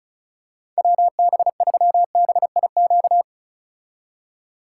Morse code, heard as a keyed tone, about 36 wpm, 700 Hz. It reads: WB3BIQ
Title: K1USN SST QSO Practice - 35wpm